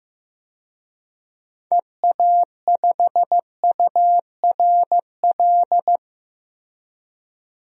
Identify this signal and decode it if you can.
Morse code, heard as a keyed tone, about 16 wpm, 700 Hz. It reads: EA5URL
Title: DXCC-100 Call Signs - 6 Characters - Repeated 2x - 15wpm